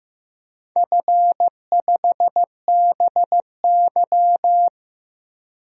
Morse code, heard as a keyed tone, about 15 wpm, 700 Hz. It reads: F5BY
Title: DXCC-100 Call Signs - 4 Characters - 15wpm